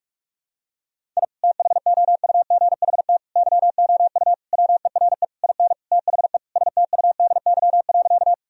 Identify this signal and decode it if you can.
Morse code, heard as a keyed tone, about 46 wpm, 700 Hz. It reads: I THOUGHT YOU WERE IN THE STUDY.
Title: Sentences from Top 1000 Words - Review Part 1 - 45wpm